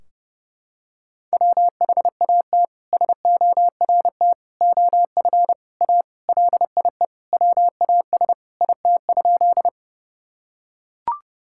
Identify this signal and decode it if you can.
Morse code, heard as a keyed tone, about 30 wpm, 700 Hz. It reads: WHAT SORT OF A LIE WAS IT?
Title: Sentences from Top 1600 Words 30wpm